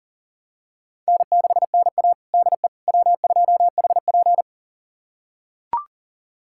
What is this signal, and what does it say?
Morse code, heard as a keyed tone, about 40 wpm, 700 Hz. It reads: N6NA DE W2HP
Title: USA Callsign DE Callsign - 4 Characters - 40wpm